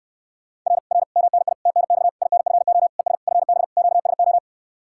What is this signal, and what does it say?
Morse code, heard as a keyed tone, about 69 wpm, 700 Hz. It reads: R R MNI TNX INFO ES FB QSO